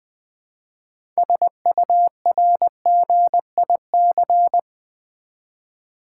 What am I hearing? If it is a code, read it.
Morse code, heard as a keyed tone, about 20 wpm, 700 Hz. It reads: SURGIC
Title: Build A Word - 2 Letters At A Time - 8-Letter Words - 20wpm